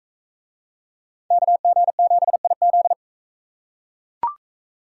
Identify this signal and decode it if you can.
Morse code, heard as a keyed tone, about 41 wpm, 700 Hz. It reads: KG7IZ